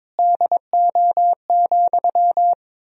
Morse code, heard as a keyed tone, about 22 wpm, 700 Hz. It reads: DO,
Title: 4x Word-Spacing - Sentences from Top 100 Words - 22wpm